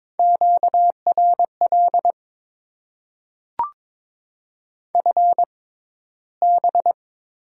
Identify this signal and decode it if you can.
Morse code, heard as a keyed tone, about 22 wpm, spaced as for 13 wpm, 700 Hz. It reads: QRL FB
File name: Top 90 QSO Elements - 6x Character Spacing - 22wpm